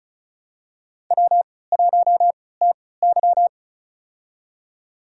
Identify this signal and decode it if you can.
Morse code, heard as a keyed tone, about 34 wpm, spaced as for 20 wpm, 700 Hz. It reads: W1TY
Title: USA Call Signs - 3x Character Spacing - 35wpm